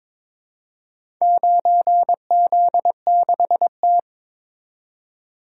Morse code, heard as a keyed tone, about 22 wpm, 700 Hz. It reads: 9Z6T
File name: practice / 22wpm / Four Letter-Number Combinations - 22wpm